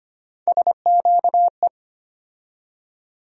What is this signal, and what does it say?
Morse code, heard as a keyed tone, about 25 wpm, 700 Hz. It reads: SQE
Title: USA Call Sign Postfixes - 3 Letters 25wpm